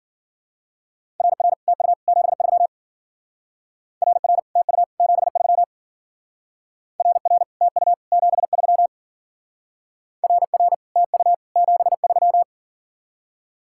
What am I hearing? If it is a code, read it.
Morse code, heard as a keyed tone, about 50 wpm, 700 Hz. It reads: RR TU 73 RR TU 73 RR TU 73 RR TU 73
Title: Speed-Racing Common QSO Phrases - 40wpm